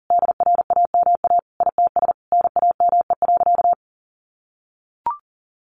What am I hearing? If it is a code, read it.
Morse code, heard as a keyed tone, about 40 wpm, 700 Hz. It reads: DRAMA ITS NAME.
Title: Sentences from Top 1500 Words 40wpm